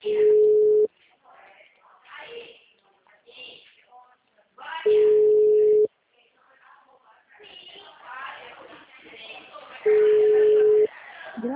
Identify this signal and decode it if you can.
Ringback tone (Europe, 425 Hz), with speech.